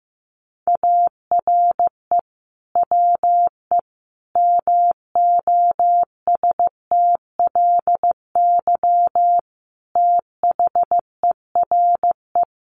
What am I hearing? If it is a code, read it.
Morse code, heard as a keyed tone, about 15 wpm, 700 Hz. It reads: ARE WE MOSTLY THERE
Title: Sentences from Top 1300 Words 15wpm